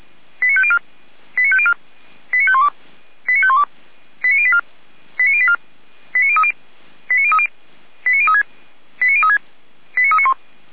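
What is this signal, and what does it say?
Five-tone selective call (ZVEI), heard as quick runs of five tones, tones of about 70 ms. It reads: 89504 89504 89512 89512 89084 89084 89030 89030 89036 89036 89391
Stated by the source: ZVEI five-tone selective call received on an SDR.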